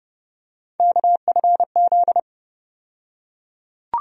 Morse code, heard as a keyed tone, about 31 wpm, 700 Hz. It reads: KFZ
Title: USA Call Sign Postfixes - 3 Letters 30wpm